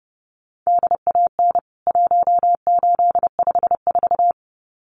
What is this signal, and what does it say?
Morse code, heard as a keyed tone, about 30 wpm, 700 Hz. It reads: DAN 1854
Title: CWT Exchange 30wpm